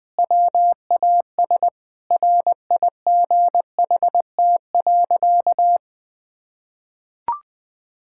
Morse code, heard as a keyed tone, about 19 wpm, 700 Hz. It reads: WAS RIGHT.
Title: Sentences with 4-Word Phrases - 20wpm